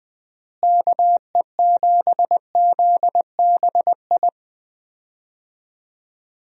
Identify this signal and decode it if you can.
Morse code, heard as a keyed tone, about 20 wpm, 700 Hz. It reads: KE7ZBI
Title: USA Call Signs - 6 Characters - Repeated 2x - 20wpm